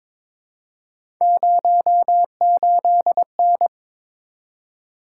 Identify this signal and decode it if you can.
Morse code, heard as a keyed tone, about 22 wpm, 700 Hz. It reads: 08N
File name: practice / 22wpm / Three Letter-Number Combinations - 22wpm